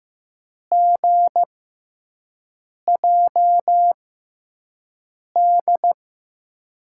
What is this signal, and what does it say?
Morse code, heard as a keyed tone, about 15 wpm, spaced as for 5 wpm, 700 Hz. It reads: GJD
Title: USA Call Signs - 5 Characters - 6x Character Spacing - 15wpm